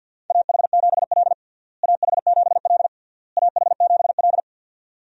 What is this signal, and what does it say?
Morse code, heard as a keyed tone, about 52 wpm, 700 Hz. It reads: AH7L AH7L AH7L
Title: USA Call Signs - 4 Characters - Repeated 3x - 50wpm